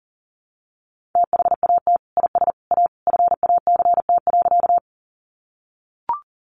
Morse code, heard as a keyed tone, about 40 wpm, 700 Hz. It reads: THAT IS A FACT.
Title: Sentences from Top 500 Words 40wpm